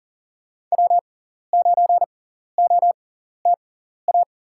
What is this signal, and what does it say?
Morse code, heard as a keyed tone, about 37 wpm, spaced as for 13 wpm, 700 Hz. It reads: W9OTA
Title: USA Call Signs - 5 Characters - 6x Character Spacing - 40wpm